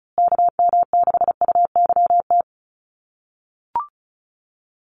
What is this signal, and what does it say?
Morse code, heard as a keyed tone, about 34 wpm, 700 Hz. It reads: KM6UYT
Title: USA Call Signs Repeated 2x 35wpm